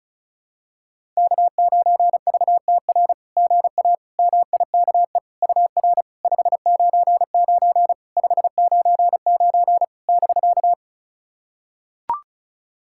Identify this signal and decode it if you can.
Morse code, heard as a keyed tone, about 36 wpm, 690 Hz. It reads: K9VTR GA MIKE UR 599 599 <BK>
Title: SOTA QSO Practice 35wpm